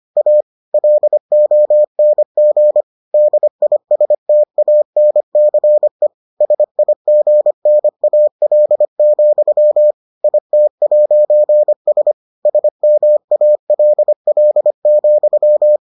Morse code, heard as a keyed tone, about 25 wpm, 600 Hz. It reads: A LONG DISTANCE SIGNAL, IT'S SMALL,